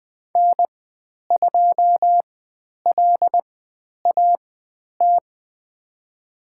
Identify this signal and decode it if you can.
Morse code, heard as a keyed tone, about 20 wpm, spaced as for 10 wpm, 700 Hz. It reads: N2LAT
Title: 109 Lesson 18g - Letter m - Call Signs 20wpm